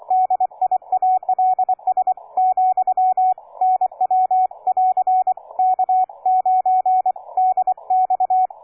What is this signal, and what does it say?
Morse code, heard as a keyed tone, about 24 wpm, 740 Hz, with radio noise. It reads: DIALS , NW <AR> K9DX
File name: cw forum / K9DX 1818 le 17-12-2008 05h47